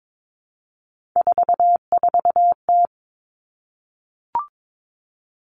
Morse code, heard as a keyed tone, about 22 wpm, 700 Hz. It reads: 44T
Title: Three Challenging Letter-Number Combinations 22wpm